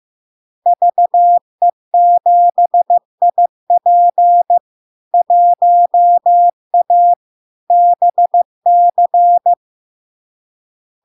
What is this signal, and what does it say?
Morse code, heard as a keyed tone, about 15 wpm, 700 Hz. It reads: VE7IP 1A BC
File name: LARK-FD-2022-Log-in-CW-15wpm-700-Hz